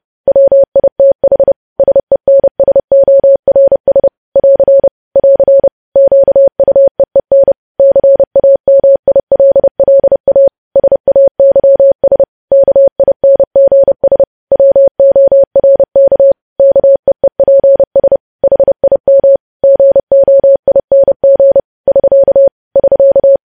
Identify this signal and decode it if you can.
Morse code, heard as a keyed tone, about 30 wpm, 550 Hz. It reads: WITH SENSORS <AR> <AR> QUEEN CAMILLA SAYS KINGS WORK KEEPS HIM GOING <SK> <SK>
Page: news headlines in morse code at 30 wpm